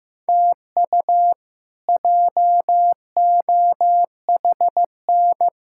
Morse code, heard as a keyed tone, about 15 wpm, 700 Hz. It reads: TU JOHN
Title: K1USN SST QSO Practice - 15wpm